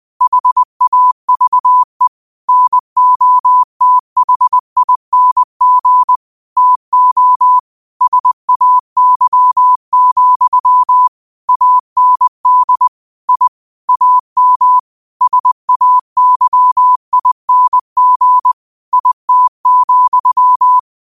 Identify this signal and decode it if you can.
Morse code, heard as a keyed tone, about 20 wpm, 1000 Hz. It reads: HAVE NOTHING TO SAY, AND I AM SAYING IT,
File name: qotdfiles / 20wpm